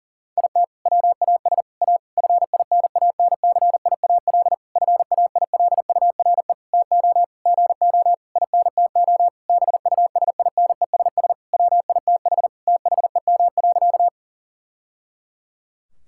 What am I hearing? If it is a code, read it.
Morse code, heard as a keyed tone, about 37 wpm, 700 Hz. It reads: IT WAS A FINANCIAL FAILURE TO GO INTO BUSINESS WITH THEM.
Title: Sentences from Top 1000 Words - Review Part 6 - 40wpm